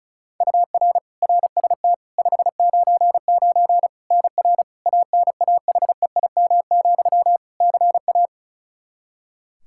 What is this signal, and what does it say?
Morse code, heard as a keyed tone, about 36 wpm, 700 Hz. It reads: UR RST 599 NR ANAHEIM, CA
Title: Practice QSO - Exchange 1 - Beginning - 35wpm